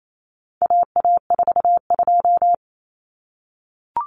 Morse code, heard as a keyed tone, about 28 wpm, 700 Hz. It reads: AA42
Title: Common Cut Numbers - 4-Digit 28wpm